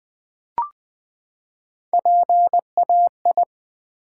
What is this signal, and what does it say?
Morse code, heard as a keyed tone, about 20 wpm, 700 Hz. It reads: PAI
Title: Build A Word - 3 Letters At A Time - 8-Letter Words - 20wpm